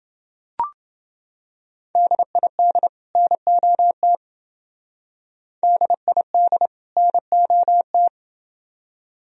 Morse code, heard as a keyed tone, about 28 wpm, 700 Hz. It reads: DID NOT DID NOT